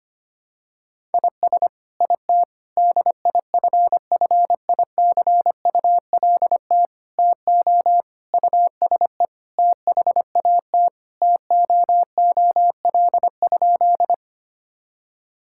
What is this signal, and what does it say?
Morse code, heard as a keyed tone, about 25 wpm, 700 Hz. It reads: IS IT DIFFICULT TO USE THAT TOOL?
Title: Sentences from Top 1000 Words - Review Part 6 - 25wpm